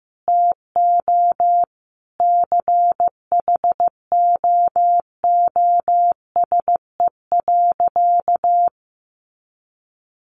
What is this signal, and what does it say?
Morse code, heard as a keyed tone, about 15 wpm, 700 Hz. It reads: TO CHOOSE.
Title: Sentences from Top 700 Words 15wpm